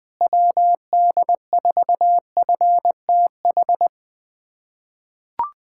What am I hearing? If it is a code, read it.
Morse code, heard as a keyed tone, about 20 wpm, 700 Hz. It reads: WD4FTH
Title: USA Callsign DE Callsign - 6 Characters - 20wpm